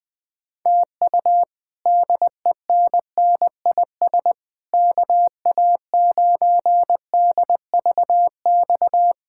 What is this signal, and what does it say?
Morse code, heard as a keyed tone, about 20 wpm, 700 Hz. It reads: TU DENNIS KA9DVX
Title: K1USN SST QSO Practice - 20wpm